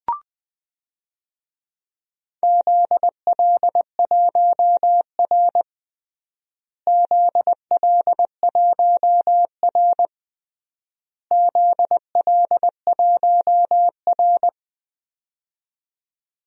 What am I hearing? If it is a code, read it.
Morse code, heard as a keyed tone, about 20 wpm, 700 Hz. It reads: ZL1R ZL1R ZL1R
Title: DXCC-100 Call Signs - 4 Characters - Repeated 3x - 20wpm